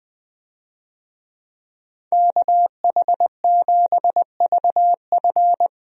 Morse code, heard as a keyed tone, about 20 wpm, 700 Hz. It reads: KH7VF
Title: USA Call Signs - 5 Characters - Repeated 2x - 20wpm